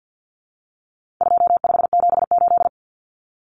Morse code, heard as a keyed tone, about 50 wpm, 700 Hz. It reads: 2578
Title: Common Cut Numbers - 4-Digit 50wpm